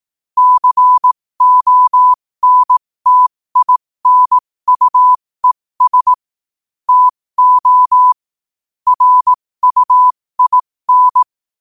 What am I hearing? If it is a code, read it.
Morse code, heard as a keyed tone, about 18 wpm, spaced as for 15 wpm, 1000 Hz. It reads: CONTINUES TO RUIN
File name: qotdfiles / 15wpm